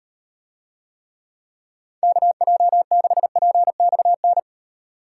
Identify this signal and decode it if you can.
Morse code, heard as a keyed tone, about 36 wpm, 700 Hz. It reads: KJ6PXN